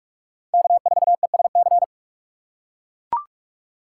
Morse code, heard as a keyed tone, about 45 wpm, 700 Hz. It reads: K4ESC
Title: USA Call Signs - 5 Characters - 2x Character Spacing - 45wpm